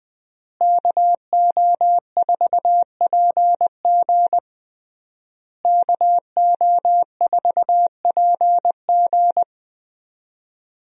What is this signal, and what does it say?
Morse code, heard as a keyed tone, about 20 wpm, 700 Hz. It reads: KO4PG KO4PG
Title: USA Call Signs - 5 Characters - Repeated 3x - 20wpm